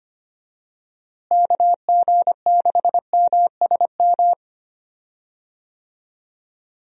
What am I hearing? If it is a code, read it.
Morse code, heard as a keyed tone, about 25 wpm, 700 Hz. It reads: KG6MSM
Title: USA Call Signs - 6 Characters - Repeated 3x - 25wpm